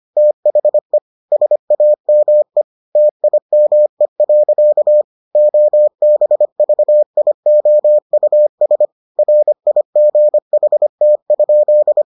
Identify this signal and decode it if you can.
Morse code, heard as a keyed tone, about 25 wpm, 600 Hz. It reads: THE SAME TIME. OBVIOUS RIGHT?